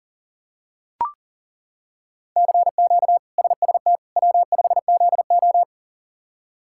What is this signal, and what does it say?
Morse code, heard as a keyed tone, about 40 wpm, 700 Hz. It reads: CQ SST W5ZO
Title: K1USN SST QSO Practice - 40wpm